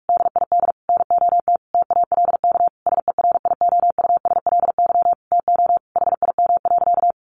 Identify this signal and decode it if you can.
Morse code, heard as a keyed tone, about 44 wpm, 700 Hz. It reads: DID NOT TALK SERIOUSLY TO HIM.
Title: Sentences from Top 1200 Words 45wpm